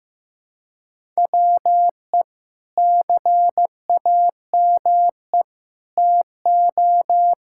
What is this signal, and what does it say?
Morse code, heard as a keyed tone, about 15 wpm, 700 Hz. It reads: WE CAME TO
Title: Sentences from Top 1000 Words - Review Part 6 - 15wpm